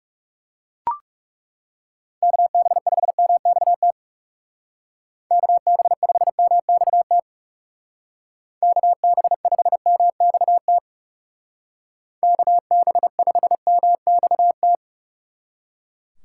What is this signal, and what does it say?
Morse code, heard as a keyed tone, about 39 wpm, 700 Hz. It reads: KB5MXT KB5MXT KB5MXT KB5MXT